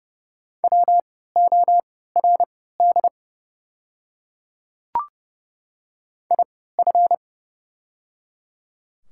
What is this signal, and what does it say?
Morse code, heard as a keyed tone, about 30 wpm, spaced as for 17 wpm, 700 Hz. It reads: WORD IF
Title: 3x Character Spacing - Top 100 Words - 30wpm